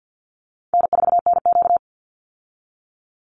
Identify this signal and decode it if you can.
Morse code, heard as a keyed tone, about 50 wpm, 700 Hz. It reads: N4NQ
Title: Easy Call Signs - 50wpm